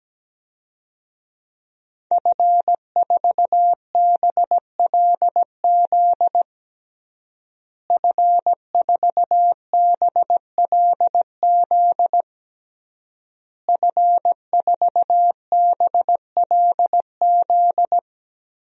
Morse code, heard as a keyed tone, about 17 wpm, 700 Hz. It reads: F4BLZ F4BLZ F4BLZ
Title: DXCC-100 Call Signs - 5 Characters - Repeated 3x - 17wpm